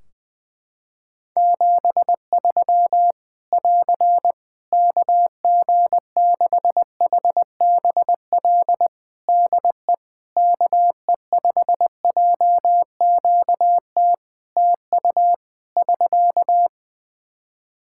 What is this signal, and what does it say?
Morse code, heard as a keyed tone, about 20 wpm, 700 Hz. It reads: 73 <AR> KG6HBL DE KE5JQT TU <SK>
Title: Practice QSO - Exchange 4 - Ending - 20wpm